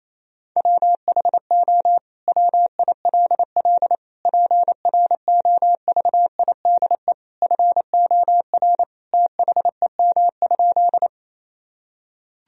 Morse code, heard as a keyed tone, about 27 wpm, 690 Hz. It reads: WHO WILL PROVIDE FOR THEM?
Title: Sentences from Top 1000 Words - Review Part 3 - 28wpm